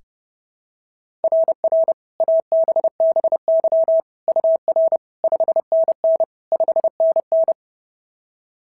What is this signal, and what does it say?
Morse code, heard as a keyed tone, about 32 wpm, 660 Hz. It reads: RR ABBY UR 5NN 5NN
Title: SOTA QSO Practice 30wpm